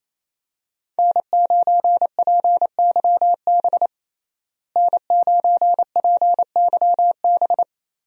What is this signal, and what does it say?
Morse code, heard as a keyed tone, about 28 wpm, 700 Hz. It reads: N9PYB N9PYB
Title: USA Call Signs - 5 Characters - Repeated 3x - 28wpm